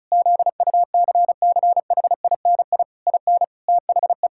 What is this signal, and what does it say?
Morse code, heard as a keyed tone, about 36 wpm, 700 Hz. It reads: ZUCCHINI IN THE
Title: Pangram Challenge v2 - 35wpm